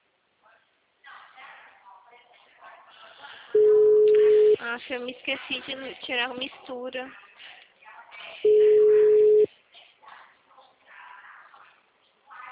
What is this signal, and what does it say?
Ringback tone (Europe, 425 Hz), with speech.
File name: out- -2041- - - .32736